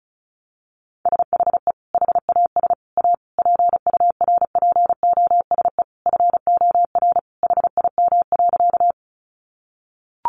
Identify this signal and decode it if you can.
Morse code, heard as a keyed tone, about 34 wpm, 700 Hz. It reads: SHE HAS A PURPOSE FOR HIM.
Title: Sentences from Top 800 Words 35wpm